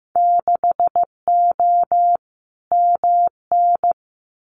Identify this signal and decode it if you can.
Morse code, heard as a keyed tone, about 15 wpm, 700 Hz. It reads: 6O MN
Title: Winter Field Day Exchanges Encore 15wpm